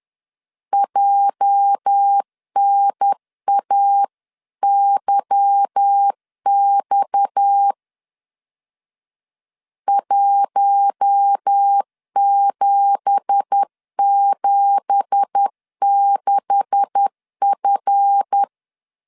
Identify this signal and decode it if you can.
Morse code, heard as a keyed tone, about 10 wpm, 800 Hz. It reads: JNAYX 1776F
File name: LEZIONI / 9 2867 10wpm 2733